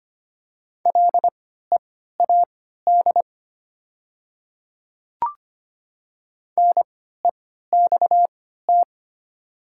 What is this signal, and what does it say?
Morse code, heard as a keyed tone, about 25 wpm, spaced as for 14 wpm, 700 Hz. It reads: LEAD NEXT
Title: 3x Character Spacing - Top 500 Words - 25wpm